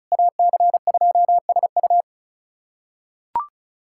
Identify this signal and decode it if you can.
Morse code, heard as a keyed tone, about 34 wpm, 700 Hz. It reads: AC2SU
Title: USA Callsign DE Callsign - 5 Characters - 35wpm